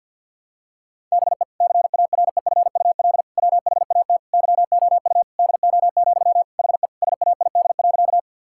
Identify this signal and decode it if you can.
Morse code, heard as a keyed tone, about 50 wpm, 700 Hz. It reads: BE CAREFUL WHAT YOU DO, HE SAID.
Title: Sentences from Top 1600 Words 50wpm